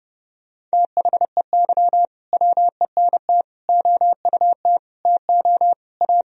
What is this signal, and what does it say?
Morse code, heard as a keyed tone, about 30 wpm, 700 Hz. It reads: THEY WENT OUT TO A
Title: Sentences from Top 1000 Words - Review Part 2 - 30wpm